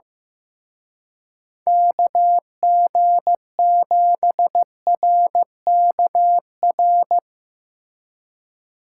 Morse code, heard as a keyed tone, about 16 wpm, 700 Hz. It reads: KG7RKR
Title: Speed Racing - USA 2x3 Call Signs 15wpm